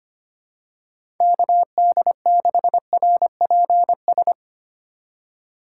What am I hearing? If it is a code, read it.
Morse code, heard as a keyed tone, about 25 wpm, 700 Hz. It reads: KD6RPS
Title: USA Call Signs - 6 Characters - 3x Character Spacing - 25wpm